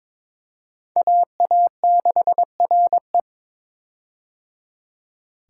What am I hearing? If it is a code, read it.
Morse code, heard as a keyed tone, about 22 wpm, 700 Hz. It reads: AA6RE
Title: K1USN SST QSO Practice - 22wpm